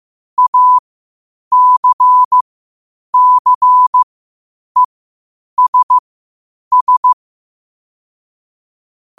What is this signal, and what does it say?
Morse code, heard as a keyed tone, about 15 wpm, spaced as for 8 wpm, 1000 Hz. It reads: ACCESS
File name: qotdfiles / 8wpm